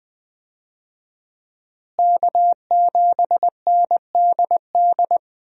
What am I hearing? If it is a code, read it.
Morse code, heard as a keyed tone, about 21 wpm, 700 Hz. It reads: K7NDD
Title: USA Call Signs - 5 Characters - Repeated 3x - 20wpm